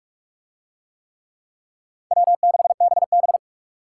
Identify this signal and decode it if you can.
Morse code, heard as a keyed tone, about 44 wpm, 700 Hz. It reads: W6BB